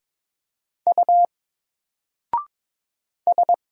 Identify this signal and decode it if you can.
Morse code, heard as a keyed tone, about 22 wpm, 700 Hz. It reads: U S
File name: Single-Letter-Number-22wpm